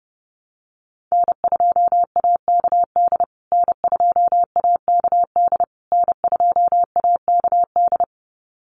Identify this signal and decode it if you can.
Morse code, heard as a keyed tone, about 30 wpm, 700 Hz. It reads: N2AKD N2AKD N2AKD
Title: USA Call Signs Repeated 3x 30wpm